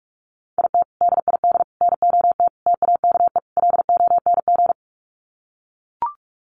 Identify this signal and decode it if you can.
Morse code, heard as a keyed tone, about 44 wpm, 700 Hz. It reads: IT DID NOT TAKE LONG
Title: Sets of 5 Words Encore2 45wpm